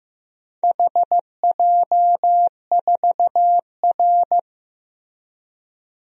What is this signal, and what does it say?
Morse code, heard as a keyed tone, about 15 wpm, 700 Hz. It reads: HJ4R
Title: DXCC-100 Call Signs - 4 Characters - Repeated 2x - 15wpm